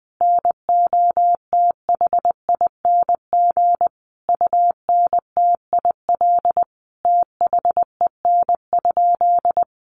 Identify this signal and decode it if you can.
Morse code, heard as a keyed tone, about 19 wpm, 700 Hz. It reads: NOTHING UNTIL THEN?
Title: Sentences from Top 500 Words 20wpm